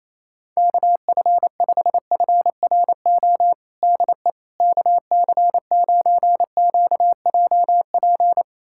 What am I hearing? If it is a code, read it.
Morse code, heard as a keyed tone, about 28 wpm, 700 Hz. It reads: KF5FRO DE KC9QJP
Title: USA Callsign DE Callsign - 6 Characters - 28wpm